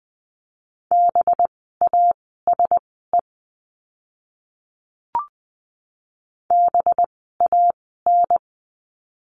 Morse code, heard as a keyed tone, about 20 wpm, spaced as for 14 wpm, 700 Hz. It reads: BASE BAN
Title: 185 Lesson 30f - Letter b - Words 20wpm